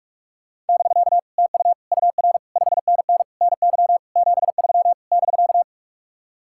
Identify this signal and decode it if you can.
Morse code, heard as a keyed tone, about 59 wpm, spaced as for 52 wpm, 700 Hz. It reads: <BK> TU UR 5NN NY 73 <BK>
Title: Speed-Racing Common QSO Phrases Encore - 45wpm